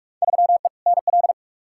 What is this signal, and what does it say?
Morse code, heard as a keyed tone, about 46 wpm, 700 Hz. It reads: 3E NL
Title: Field Day Exchanges Encore - 45wpm